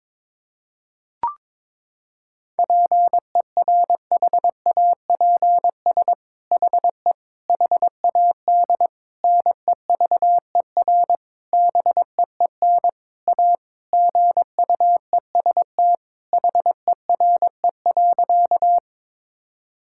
Morse code, heard as a keyed tone, about 22 wpm, 700 Hz. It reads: PERHAPS HE HAD NEVER BEEN A GUEST HERE.
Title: Sentences from Top 1600 Words 22wpm